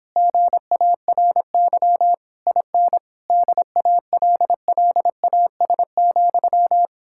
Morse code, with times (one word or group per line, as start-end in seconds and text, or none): 0.2-2.1 GARY
2.5-3.0 IN
3.3-6.9 DALLAS,